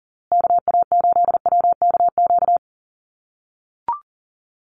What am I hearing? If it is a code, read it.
Morse code, heard as a keyed tone, about 40 wpm, 700 Hz. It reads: KA8WKQ
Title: USA Call Signs Repeated 2x 40wpm